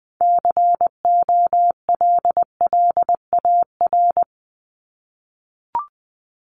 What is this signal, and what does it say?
Morse code, heard as a keyed tone, about 20 wpm, 700 Hz. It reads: COLLAR
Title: Top 1751-2000 Words 20wpm